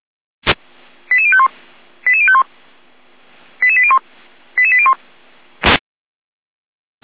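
Five-tone selective call (ZVEI), heard as quick runs of five tones, tones of about 70 ms. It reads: 88051 88051 88801 88801
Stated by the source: ZVEI five-tone call 88111 received by SDR.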